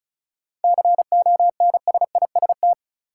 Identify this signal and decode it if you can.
Morse code, heard as a keyed tone, about 36 wpm, 700 Hz. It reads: CONSIST
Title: Sentences from Top 1400 Words - New Word Drill - 35wpm